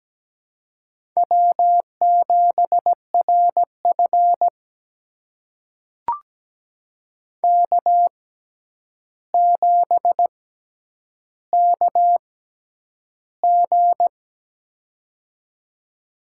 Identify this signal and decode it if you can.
Morse code, heard as a keyed tone, about 17 wpm, spaced as for 9 wpm, 700 Hz. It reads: W7RF K7KG
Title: USA Call Signs - 6x Character Spacing - 17wpm